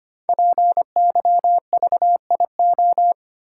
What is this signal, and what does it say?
Morse code, heard as a keyed tone, about 25 wpm, 700 Hz. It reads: PYVIO
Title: Code Groups - 5 Letter:Numbers - 25wpm